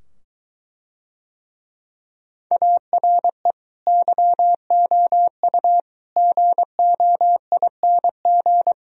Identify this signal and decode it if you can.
Morse code, heard as a keyed tone, about 22 wpm, 700 Hz. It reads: ARE YOU GOING